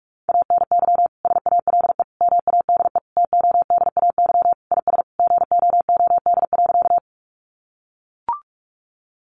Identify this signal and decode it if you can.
Morse code, heard as a keyed tone, about 44 wpm, 700 Hz. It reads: ANY SALE MADE TODAY IS GOOD.
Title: Sentences from Top 1400 Words 45wpm